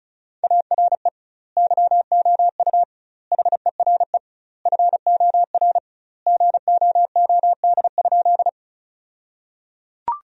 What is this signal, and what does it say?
Morse code, heard as a keyed tone, about 34 wpm, 700 Hz. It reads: ARE YOU HERE FOR GOOD?
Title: 2x Word-Spacing - Sentences from Top 200 Words - 35wpm